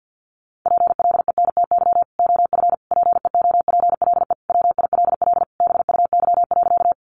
Morse code, heard as a keyed tone, about 50 wpm, 700 Hz. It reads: PLENTY OF PEOPLE WILL BUY.
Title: Sentences from Top 1300 Words 50wpm